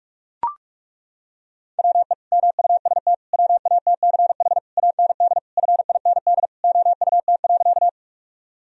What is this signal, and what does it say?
Morse code, heard as a keyed tone, about 41 wpm, 700 Hz. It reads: WE MUST WATCH AND FIND OUT.
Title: Sentences from Top 1000 Words - Review Part 1 - 45wpm